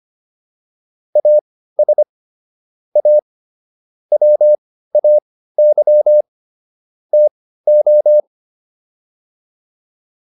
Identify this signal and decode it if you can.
Morse code, heard as a keyed tone, about 25 wpm, spaced as for 15 wpm, 600 Hz. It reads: AS A WAY TO